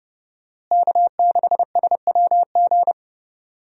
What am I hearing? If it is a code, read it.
Morse code, heard as a keyed tone, about 30 wpm, 700 Hz. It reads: K6SWG
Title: USA Call Signs - 5 Characters - Repeated 3x - 30wpm